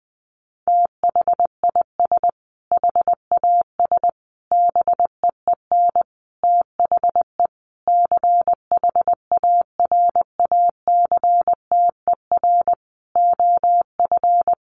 Morse code, heard as a keyed tone, about 20 wpm, 700 Hz. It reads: THIS HAS BEEN THE CHARACTER OF
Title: Sentences from Top 1400 Words 20wpm